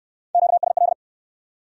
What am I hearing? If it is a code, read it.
Morse code, heard as a keyed tone, about 65 wpm, 700 Hz. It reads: QSL